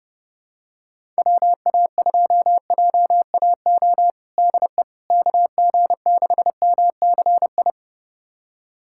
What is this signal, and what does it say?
Morse code, heard as a keyed tone, about 32 wpm, 700 Hz. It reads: WA2JAO DE KG6MCI
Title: USA Callsign DE Callsign 30wpm